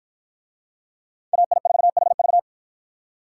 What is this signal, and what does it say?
Morse code, heard as a keyed tone, about 52 wpm, 700 Hz. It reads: AI4HV